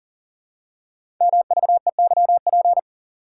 Morse code, heard as a keyed tone, about 42 wpm, 690 Hz. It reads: MVEYP
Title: Code Groups - 5 Letters - 40wpm